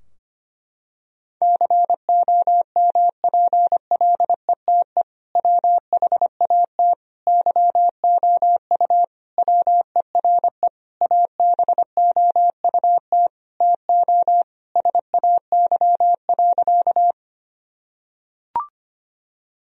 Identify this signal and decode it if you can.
Morse code, heard as a keyed tone, about 25 wpm, 700 Hz. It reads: COMPLETE WHAT YOU WERE ABOUT TO SAY.
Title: Sentences from Top 1000 Words - Review Part 2 - 25wpm